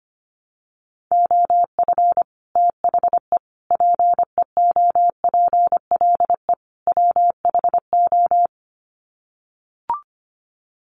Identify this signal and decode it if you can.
Morse code, heard as a keyed tone, about 25 wpm, 700 Hz. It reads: OF THE PEOPLE WHO
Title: Sets of 4 Words 25wpm